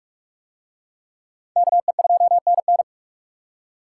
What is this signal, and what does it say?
Morse code, heard as a keyed tone, about 44 wpm, 700 Hz. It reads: KE2NN